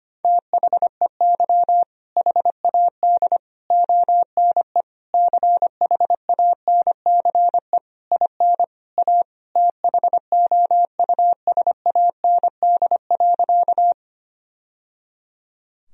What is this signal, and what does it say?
Morse code, heard as a keyed tone, about 25 wpm, 700 Hz. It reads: THEY HAD ONE CHANCE IN A THOUSAND.
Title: Sentences from Top 1000 Words - Review Part 3 - 25wpm